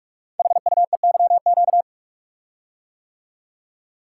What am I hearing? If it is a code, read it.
Morse code, heard as a keyed tone, about 44 wpm, 700 Hz. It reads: HUEYQ
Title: Code Groups - 5 Letters - 45wpm